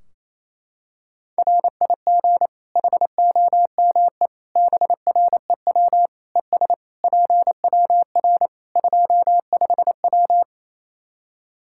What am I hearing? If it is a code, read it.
Morse code, heard as a keyed tone, about 28 wpm, 700 Hz. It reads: RIG HOME BREW ES PWR 25W
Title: Practice QSO - Exchange 2 - Info - 28wpm